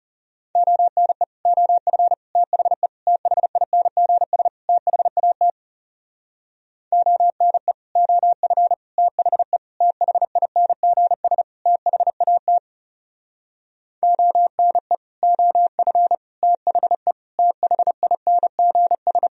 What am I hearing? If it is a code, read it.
Morse code, heard as a keyed tone, about 37 wpm, 700 Hz. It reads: ONE OF THE THINGS THAT ONE OF THE THINGS THAT ONE OF THE THINGS
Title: Speed-Racing - Sets of 5 Words 30wpm